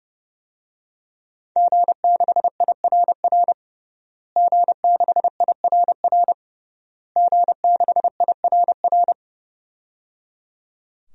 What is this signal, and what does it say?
Morse code, heard as a keyed tone, about 30 wpm, 700 Hz. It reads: G6IRR G6IRR G6IRR
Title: DXCC-100 Call Signs - 5 Characters - Repeated 3x - 30wpm